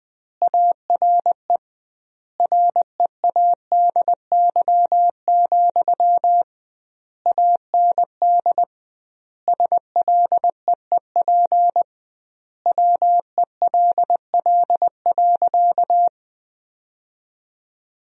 Morse code, heard as a keyed tone, about 20 wpm, 700 Hz. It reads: ARE READY, AND SLEEP WELL.
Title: 2x Word-Spacing - Sentences from Top 500 Words - 20wpm